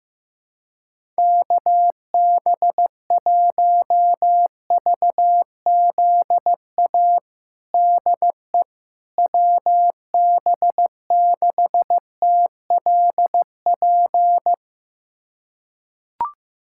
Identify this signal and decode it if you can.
Morse code, heard as a keyed tone, about 15 wpm, 700 Hz. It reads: KB1VZA DE WB6TLP
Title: USA Callsign DE Callsign - 6 Characters - 15wpm